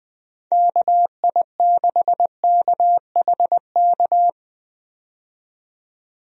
Morse code, heard as a keyed tone, about 21 wpm, 700 Hz. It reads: KI6KHK
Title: USA Call Signs - 6 Characters - 20wpm